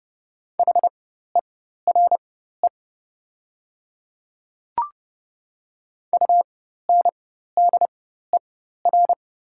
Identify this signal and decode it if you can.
Morse code, heard as a keyed tone, about 29 wpm, spaced as for 14 wpm, 700 Hz. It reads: HERE UNDER
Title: 4x Character Spacing - Top 200 Words - 30wpm